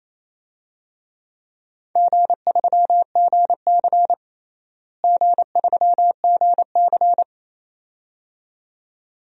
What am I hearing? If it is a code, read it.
Morse code, heard as a keyed tone, about 28 wpm, 700 Hz. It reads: G3GC G3GC
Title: DXCC-100 Call Signs - 4 Characters - Repeated 2x - 28wpm